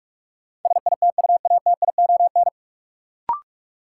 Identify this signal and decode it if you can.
Morse code, heard as a keyed tone, about 46 wpm, 700 Hz. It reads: SITUATION
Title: Sentences from Top 1200 Words - New Word Drill Alt - 45wpm